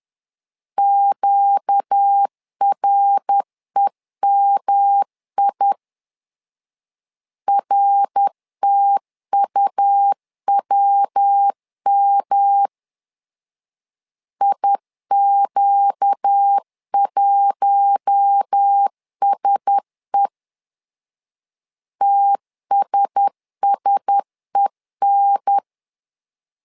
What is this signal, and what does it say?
Morse code, heard as a keyed tone, about 10 wpm, 800 Hz. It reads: QREMI RTUWM IQ1SE TSSEN
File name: LEZIONI / 4 Q1S 10wpm 2133